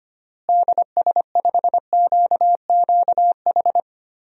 Morse code, heard as a keyed tone, about 25 wpm, 700 Hz. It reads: DS5QQH
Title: DXCC-100 Call Signs - 6 Characters - Repeated 3x - 25wpm